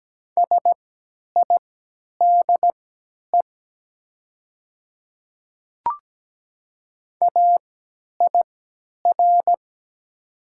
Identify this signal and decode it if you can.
Morse code, heard as a keyed tone, about 17 wpm, spaced as for 10 wpm, 700 Hz. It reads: SIDE AIR
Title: 3x Character Spacing - Top 300 Words - 17wpm